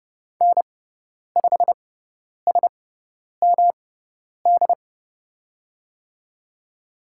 Morse code, heard as a keyed tone, about 30 wpm, spaced as for 10 wpm, 690 Hz. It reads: N5SMD
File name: Lesson 18g - Letter m - Call Signs 30wpm